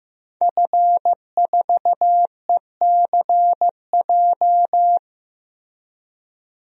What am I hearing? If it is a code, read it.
Morse code, heard as a keyed tone, about 15 wpm, 700 Hz. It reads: F4ECJ
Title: DXCC-100 Call Signs - 5 Characters - Repeated 2x - 15wpm